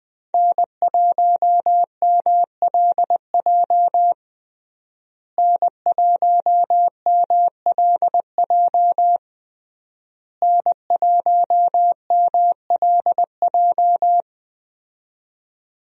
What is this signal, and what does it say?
Morse code, heard as a keyed tone, about 20 wpm, 700 Hz. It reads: N1MLJ N1MLJ N1MLJ
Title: USA Call Signs - 5 Characters - Repeated 3x - 20wpm